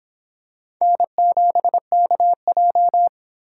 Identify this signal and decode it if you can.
Morse code, heard as a keyed tone, about 25 wpm, 700 Hz. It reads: N7KJ